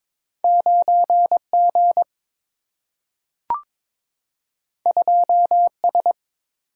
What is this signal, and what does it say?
Morse code, heard as a keyed tone, about 22 wpm, 690 Hz. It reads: 9G 2S
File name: practice / 22wpm / Two Letter-Number Combinations - 22wpm